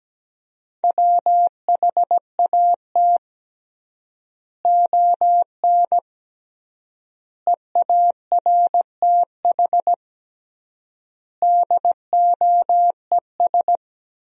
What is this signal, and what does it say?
Morse code, heard as a keyed tone, about 17 wpm, 700 Hz. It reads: WHAT ON EARTH DOES
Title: 3x Word-Spacing - Sentences from Top 200 Words - 17wpm